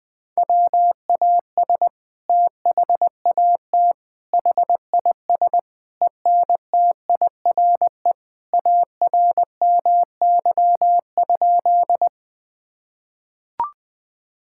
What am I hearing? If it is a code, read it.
Morse code, heard as a keyed tone, about 20 wpm, 700 Hz. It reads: WAS THAT HIS ENTIRE ARMY?
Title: Sentences from Top 1000 Words - Review Part 5 - 20wpm